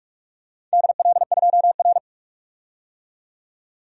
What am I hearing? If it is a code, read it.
Morse code, heard as a keyed tone, about 46 wpm, 700 Hz. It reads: DL2R
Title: DXCC-100 Call Signs - 4 Characters - 45wpm